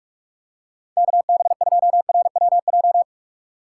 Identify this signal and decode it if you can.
Morse code, heard as a keyed tone, about 46 wpm, 700 Hz. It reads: KB2RWJ
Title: USA Call Signs - 6 Characters - 4x Character Spacing - 45wpm